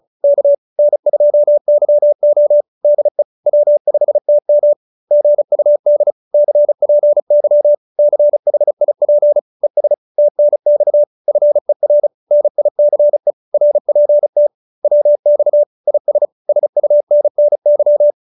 Morse code, heard as a keyed tone, about 34 wpm, 590 Hz. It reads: K N2YO DE W5TM GUD CPY CHIP ES TNX FER NICE RPT WX IS SUNNY